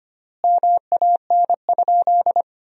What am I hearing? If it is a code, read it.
Morse code, heard as a keyed tone, about 25 wpm, 700 Hz. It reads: MAN?
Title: 4x Word-Spacing - Sentences from Top 200 Words - 25wpm